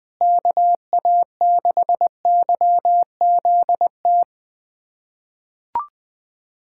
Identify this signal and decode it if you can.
Morse code, heard as a keyed tone, about 20 wpm, 700 Hz. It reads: KA6YZT
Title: USA Callsign DE Callsign - 6 Characters - 20wpm